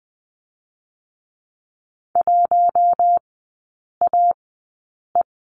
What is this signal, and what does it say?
Morse code, heard as a keyed tone, about 20 wpm, spaced as for 8 wpm, 700 Hz. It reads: 1AE
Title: 041 Lesson 8e - Number 1 - Three Characters 20wpm